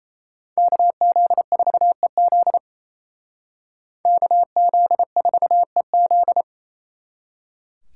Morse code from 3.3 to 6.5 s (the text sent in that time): KZ4EZ